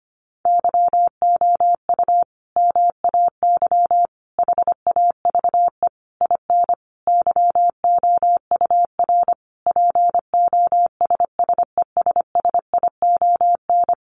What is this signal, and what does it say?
Morse code, heard as a keyed tone, about 25 wpm, 700 Hz. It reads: YOU MAY HAVE IN YOUR POSSESSION
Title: Sentences from Top 1400 Words 25wpm